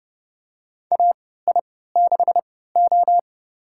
Morse code, heard as a keyed tone, about 32 wpm, spaced as for 17 wpm, 700 Hz. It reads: AI6O
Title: USA Call Signs - 4 Characters - 3x Character Spacing - 30wpm